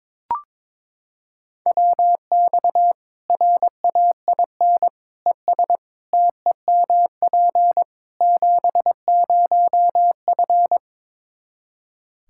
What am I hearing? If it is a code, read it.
Morse code, heard as a keyed tone, about 22 wpm, 700 Hz. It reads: WX RAIN ES TEMP 70F
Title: Practice QSO - Exchange 2 - Info - 22wpm